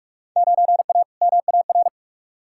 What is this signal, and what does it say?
Morse code, heard as a keyed tone, about 46 wpm, 700 Hz. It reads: 9A MAR
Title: Field Day Exchanges Encore - 45wpm